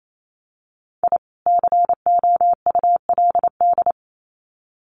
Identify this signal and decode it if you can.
Morse code, heard as a keyed tone, about 28 wpm, 700 Hz. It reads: I COULD
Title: Sets of 2 Words 28wpm